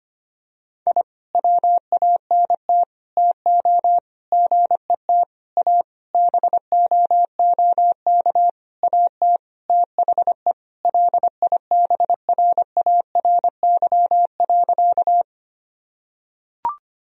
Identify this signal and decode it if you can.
Morse code, heard as a keyed tone, about 25 wpm, 700 Hz. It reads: I WANT TO GET A BOOK AT THE LIBRARY.
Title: Sentences from Top 1000 Words - Review Part 1 - 25wpm